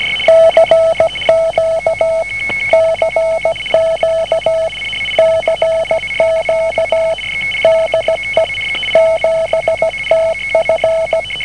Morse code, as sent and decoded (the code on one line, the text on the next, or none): -.-. --.- / -.-. --.- / -.-. --.- / -.. / --... - ..-.
CQ CQ CQ DE 7TF